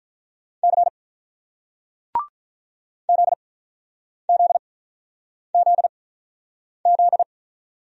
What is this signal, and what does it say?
Morse code, heard as a keyed tone, about 46 wpm, spaced as for 9 wpm, 700 Hz. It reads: C ZZZZ